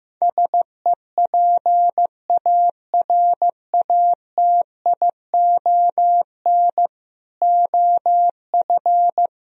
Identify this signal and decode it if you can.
Morse code, heard as a keyed tone, about 15 wpm, 700 Hz. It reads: SEPARATION OF
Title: Speed-Racing - Sets of 5 Words 15wpm